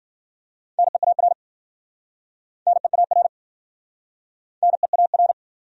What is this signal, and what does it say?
Morse code, heard as a keyed tone, about 55 wpm, 700 Hz. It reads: NEAR NEAR NEAR